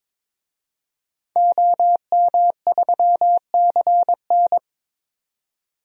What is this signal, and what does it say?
Morse code, heard as a keyed tone, about 22 wpm, 700 Hz. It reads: OM3CN
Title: DXCC-100 Call Signs - 5 Characters - 22wpm